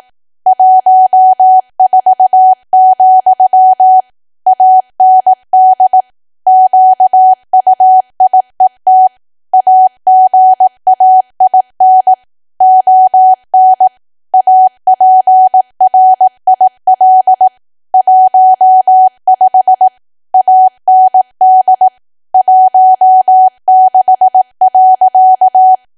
Morse code, heard as a keyed tone, about 18 wpm, 750 Hz. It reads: 14, AND QUIET AGAIN ON APRIL 15 AND 16.
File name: cwa / W1AW-Level2 / 18WPM